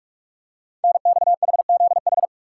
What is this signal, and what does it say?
Morse code, heard as a keyed tone, about 44 wpm, 700 Hz. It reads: NXHZH
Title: Code Groups - 5 Letters - 45wpm